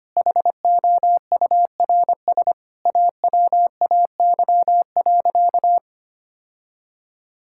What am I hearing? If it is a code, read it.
Morse code, heard as a keyed tone, about 25 wpm, 700 Hz. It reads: HOURS AWAY.
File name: practice / 25wpm / Sentences with 4-Word Phrases - 25wpm